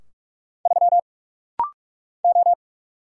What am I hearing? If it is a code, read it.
Morse code, heard as a keyed tone, about 44 wpm, 700 Hz. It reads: 3 O
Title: Single Letter-Number Alt - Rapid-Fire 45wpm